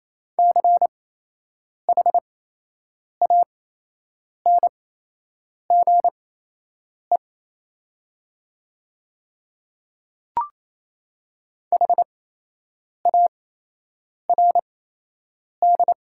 Morse code, heard as a keyed tone, about 28 wpm, spaced as for 8 wpm, 700 Hz. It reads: CHANGE HARD
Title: 8x Character Spacing - Top 300 Words - 28wpm